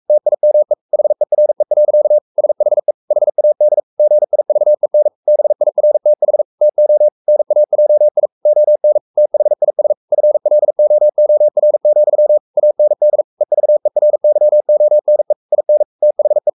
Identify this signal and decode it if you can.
Morse code, heard as a keyed tone, about 44 wpm, 600 Hz. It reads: TIME HERE. SHE HAD GIVEN BIRTH TO NAJI ON THIS FLOOR, AND EVERYONE IN THE
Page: Romanen Never af Ken Follet på engelsk.